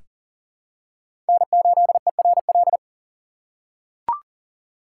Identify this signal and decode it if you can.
Morse code, heard as a keyed tone, about 40 wpm, 700 Hz. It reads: N8ERL
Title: USA Call Signs - 5 Characters - 3x Character Spacing - 40wpm